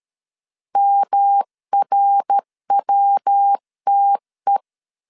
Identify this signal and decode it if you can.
Morse code, heard as a keyed tone, about 13 wpm, 800 Hz. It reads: MRWTE
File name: CW / DET / 1 3 1500C S A 12wpm 2905